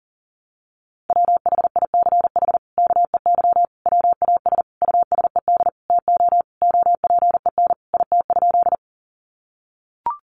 Morse code, heard as a keyed tone, about 40 wpm, 700 Hz. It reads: WHICH KEY WAS USED TO OPEN IT?
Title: Sentences from Top 600 Words 40wpm